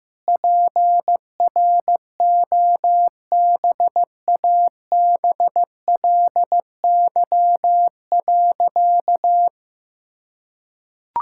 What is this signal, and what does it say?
Morse code, heard as a keyed tone, about 15 wpm, 700 Hz. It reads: PROBABLY.
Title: Sentences from Top 1000 Words - Review Part 5 - 15wpm